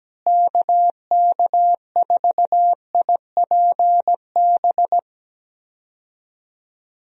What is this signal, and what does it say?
Morse code, heard as a keyed tone, about 17 wpm, 700 Hz. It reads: KK4IPB
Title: USA Call Signs - 6 Characters - Repeated 2x - 17wpm